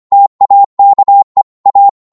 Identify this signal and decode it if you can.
Morse code, heard as a keyed tone, about 25 wpm, 800 Hz. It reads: TAKE A